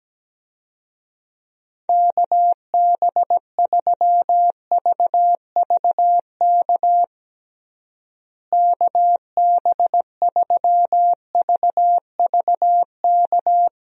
Morse code, heard as a keyed tone, about 17 wpm, 700 Hz. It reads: KB3VVK KB3VVK
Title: USA Call Signs - 6 Characters - Repeated 3x - 17wpm